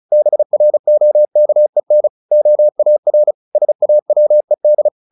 Morse code, heard as a keyed tone, about 34 wpm, 600 Hz. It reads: BROKEN OAR SAWED
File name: the old man and the sea 2 35wpm